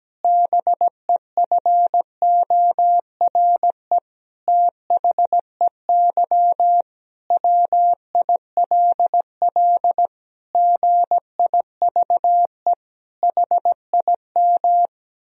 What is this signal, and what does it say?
Morse code, heard as a keyed tone, about 17 wpm, 700 Hz. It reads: BEFORE THEY WILL GIVE HIM
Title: Sentences from Top 1000 Words - Review Part 4 - 17wpm